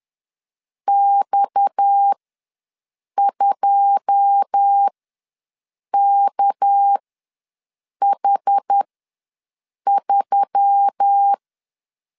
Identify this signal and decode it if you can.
Morse code, heard as a keyed tone, about 10 wpm, spaced as for 6 wpm, 800 Hz. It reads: X2KH3
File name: LEZIONI / 10 SIGNS / 10B INT ws 10wpm 4134